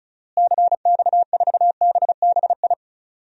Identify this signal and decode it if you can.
Morse code, heard as a keyed tone, about 36 wpm, 700 Hz. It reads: CX4BBI
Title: DXCC-100 Call Signs - 6 Characters - Repeated 2x - 35wpm